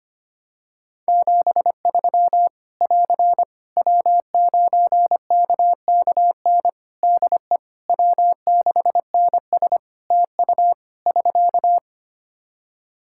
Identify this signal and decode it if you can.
Morse code, heard as a keyed tone, about 25 wpm, 700 Hz. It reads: 73 <AR> W9KKN DE W6NS TU <SK>
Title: Practice QSO - Exchange 4 - Ending - 25wpm